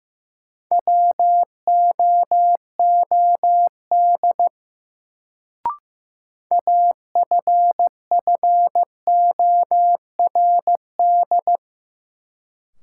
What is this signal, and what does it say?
Morse code, heard as a keyed tone, about 15 wpm, 700 Hz. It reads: WOOD AFFORD
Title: Sentences from Top 1100 Words - New Word Drill Alt - 15wpm